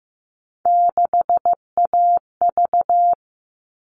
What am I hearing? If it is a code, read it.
Morse code, heard as a keyed tone, about 15 wpm, 700 Hz. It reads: 6AV
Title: All Cut Numbers - 3-Digit 15wpm